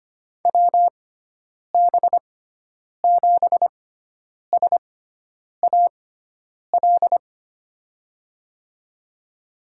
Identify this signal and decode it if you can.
Morse code, heard as a keyed tone, about 25 wpm, spaced as for 9 wpm, 700 Hz. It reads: WB7SAL
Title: USA Call Signs - 6x Character Spacing - 25wpm